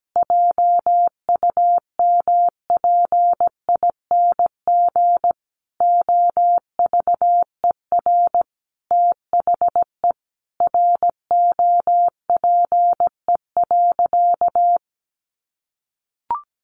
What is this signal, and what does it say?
Morse code, heard as a keyed tone, about 17 wpm, 700 Hz. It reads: JUMPING OVER THE ROPE.
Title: Easy Phrases 17wpm